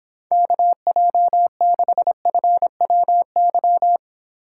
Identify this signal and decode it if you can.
Morse code, heard as a keyed tone, about 26 wpm, 700 Hz. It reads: KJ6FWY